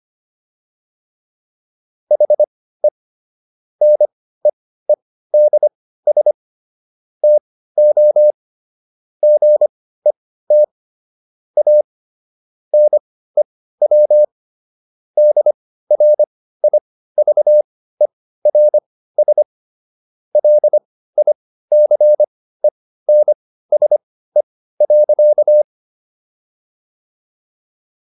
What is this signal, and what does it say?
Morse code, heard as a keyed tone, about 25 wpm, spaced as for 15 wpm, 600 Hz. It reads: HE NEEDS TO GET A NEW DRIVERS LICENSE.